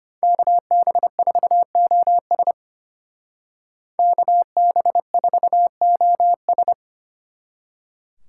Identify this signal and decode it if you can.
Morse code, heard as a keyed tone, about 33 wpm, 700 Hz. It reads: KB4OS KB4OS